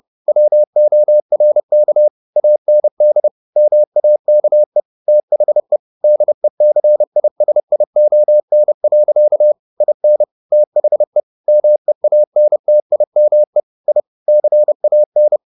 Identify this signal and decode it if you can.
Morse code, heard as a keyed tone, about 30 wpm, 600 Hz. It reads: WORK AND MAKE THE DECISION. IN THE MEANTIME I CAN